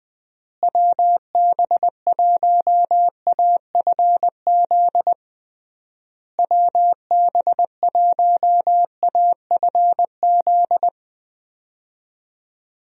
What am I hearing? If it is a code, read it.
Morse code, heard as a keyed tone, about 21 wpm, 700 Hz. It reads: WB1AFZ WB1AFZ
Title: USA Call Signs - 6 Characters - Repeated 3x - 20wpm